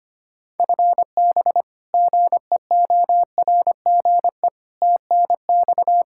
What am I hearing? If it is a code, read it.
Morse code, heard as a keyed tone, about 25 wpm, 700 Hz. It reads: FB GEORGE TNX
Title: K1USN SST QSO Practice 25wpm